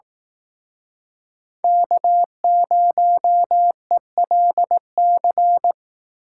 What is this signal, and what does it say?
Morse code, heard as a keyed tone, about 19 wpm, 700 Hz. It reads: K0ELC
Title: Speed Racing - USA Call Signs 15wpm